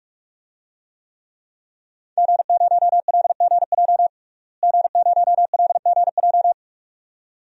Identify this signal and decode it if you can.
Morse code, heard as a keyed tone, about 46 wpm, 700 Hz. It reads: G0LGJ G0LGJ
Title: DXCC-100 Call Signs - 5 Characters - Repeated 2x - 45wpm